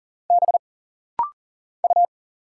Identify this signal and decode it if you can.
Morse code, heard as a keyed tone, about 40 wpm, 700 Hz. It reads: B U
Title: Single Letter-Number Alt - Rapid-Fire 40wpm